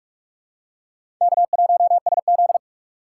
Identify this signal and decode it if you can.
Morse code, heard as a keyed tone, about 45 wpm, 700 Hz. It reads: K1SZ
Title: USA Call Signs - 2x Character Spacing - 45wpm